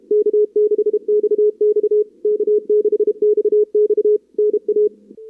Morse code, heard as a keyed tone, about 34 wpm, 440 Hz. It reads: K6XX K6XX NA
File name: CWSprint / Feb2022 / k5tr-feb-na-sprint-cw-k5ot -0100